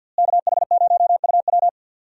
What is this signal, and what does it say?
Morse code, heard as a keyed tone, about 50 wpm, 700 Hz. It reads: KH0UW
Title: USA Call Signs - 5 Characters - Repeated 3x - 50wpm